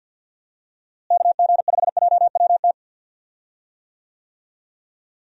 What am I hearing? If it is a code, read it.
Morse code, heard as a keyed tone, about 50 wpm, 700 Hz. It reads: KG5JWT
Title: USA Call Signs - 6 Characters - Repeated 3x - 50wpm